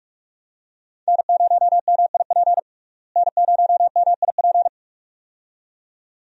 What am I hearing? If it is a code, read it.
Morse code, heard as a keyed tone, about 46 wpm, 700 Hz. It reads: N0MIP N0MIP
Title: USA Call Signs - 5 Characters - Repeated 2x - 45wpm